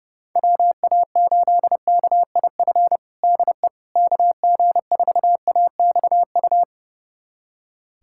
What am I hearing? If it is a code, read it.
Morse code, heard as a keyed tone, about 30 wpm, 700 Hz. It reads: WA8KIF DE KG4AXU
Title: USA Callsign DE Callsign - 6 Characters - 30wpm